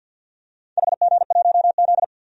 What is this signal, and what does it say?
Morse code, heard as a keyed tone, about 52 wpm, 700 Hz. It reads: HG1Z
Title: DXCC-100 Call Signs - 4 Characters - 50wpm